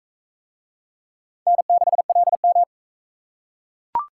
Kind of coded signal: Morse code

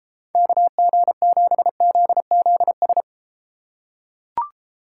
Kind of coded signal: Morse code

